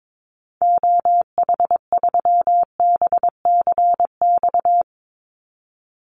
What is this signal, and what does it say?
Morse code, heard as a keyed tone, about 22 wpm, 700 Hz. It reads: OH3BCX
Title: Call Signs Repeated 3x - 22wpm